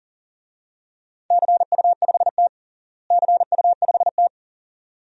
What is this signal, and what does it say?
Morse code, heard as a keyed tone, about 40 wpm, 700 Hz. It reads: CU5T CU5T
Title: DXCC-100 Call Signs - 4 Characters - Repeated 2x - 40wpm